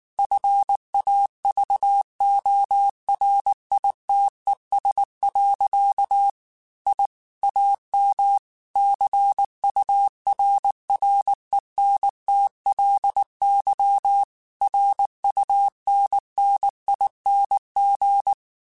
Morse code, heard as a keyed tone, about 19 wpm, 800 Hz. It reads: FAVORITES. I AM CURRENTLY RUNNING
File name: QSOs / QSOs-mp3s / WD0FBT 1@18